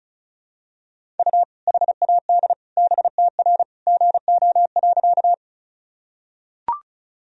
Morse code, heard as a keyed tone, about 36 wpm, 700 Hz. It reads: U HAD BTR GO.
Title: Sentences from Top 400 Words and Abbreviations 35wpm